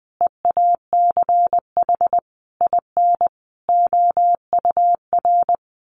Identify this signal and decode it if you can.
Morse code, heard as a keyed tone, about 20 wpm, 700 Hz. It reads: EACH IN OUR
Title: Sentences from Top 800 Words 20wpm